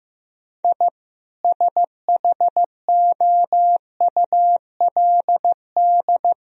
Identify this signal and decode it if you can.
Morse code, heard as a keyed tone, about 15 wpm, 700 Hz. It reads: I SHOULD
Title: Sentences from Top 1000 Words - Review Part 3 - 15wpm